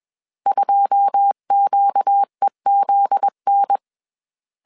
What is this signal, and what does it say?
Morse code, heard as a keyed tone, about 21 wpm, 800 Hz. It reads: 2QEZN